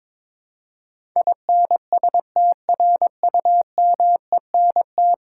Morse code, heard as a keyed tone, about 22 wpm, 700 Hz. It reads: INSTRUMENT
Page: morse-code-ninja-practice-22wpm